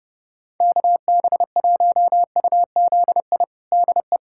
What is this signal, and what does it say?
Morse code, heard as a keyed tone, about 30 wpm, 700 Hz. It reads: KB1UZI DE
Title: USA Callsign DE Callsign 30wpm